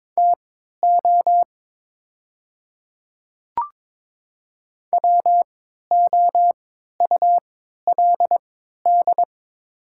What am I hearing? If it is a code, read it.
Morse code, heard as a keyed tone, about 22 wpm, spaced as for 12 wpm, 700 Hz. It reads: TO WOULD
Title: 3x Character Spacing - Top 100 Words - 22wpm